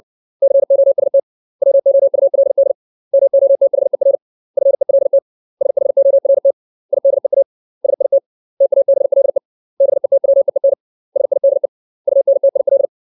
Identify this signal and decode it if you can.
Morse code, heard as a keyed tone, about 60 wpm, 550 Hz. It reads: COST WOULD MOTHER FELT SHORT IDEA SIT TABLE BETWEEN SIDE UNTIL
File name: 632 words / 632 Words 60WPM